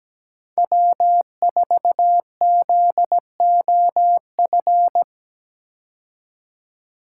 Morse code, heard as a keyed tone, about 17 wpm, 700 Hz. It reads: W4ZOF
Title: Code Groups - 5 Letter:Numbers - 17wpm